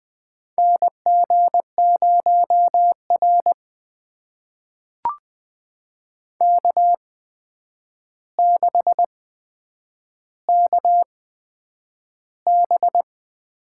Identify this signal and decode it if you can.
Morse code, heard as a keyed tone, about 20 wpm, spaced as for 9 wpm, 700 Hz. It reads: NG0R K6KB
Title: USA Call Signs - 8x Character Spacing - 20wpm